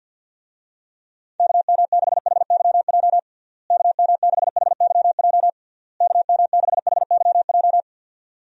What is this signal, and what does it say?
Morse code, heard as a keyed tone, about 47 wpm, 700 Hz. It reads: KM6HYJ KM6HYJ KM6HYJ
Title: USA Call Signs - 6 Characters - Repeated 3x - 50wpm